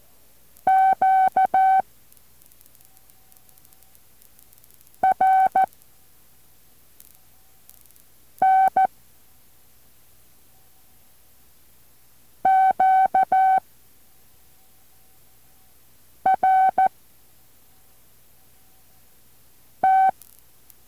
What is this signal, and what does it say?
Morse code, heard as a keyed tone, about 14 wpm, 760 Hz. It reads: Q R N Q R T
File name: intl morse code2